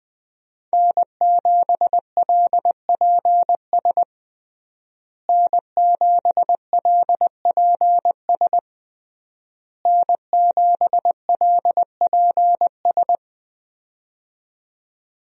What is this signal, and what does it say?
Morse code, heard as a keyed tone, about 21 wpm, 700 Hz. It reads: N7LPS N7LPS N7LPS
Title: USA Call Signs - 5 Characters - Repeated 3x - 20wpm